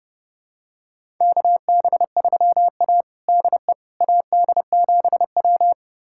Morse code, heard as a keyed tone, about 32 wpm, 700 Hz. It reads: KB3A DE AD7W
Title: USA Callsign DE Callsign - 4 Characters - 30wpm